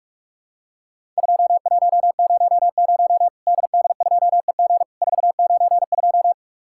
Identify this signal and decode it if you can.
Morse code, heard as a keyed tone, about 43 wpm, 700 Hz. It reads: 2100 DD2EG 492
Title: Worked All Europe DX Contest - QTC Exchange - 45wpm